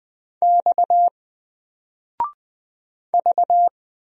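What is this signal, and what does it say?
Morse code, heard as a keyed tone, about 20 wpm, 700 Hz. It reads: X V
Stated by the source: Single letter and number Morse drill at 20 wpm, from N4LCD.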